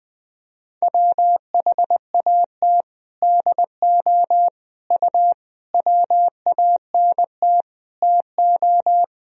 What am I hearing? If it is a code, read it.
Morse code, heard as a keyed tone, about 20 wpm, 700 Hz. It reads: WHAT DO U WANT TO
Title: Sentences from Top 200 Words and Abbreviations 20wpm